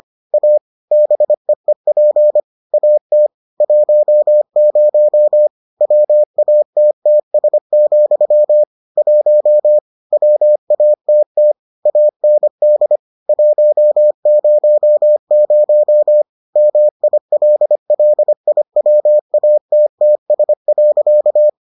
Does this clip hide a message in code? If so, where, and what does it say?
Yes: Morse code from 0.3 to 21.5 s, heard as a keyed tone, about 25 wpm, 600 Hz. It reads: A BEEP AT 10 WATTS, 1 WATT AND 100 MILLIWATTS.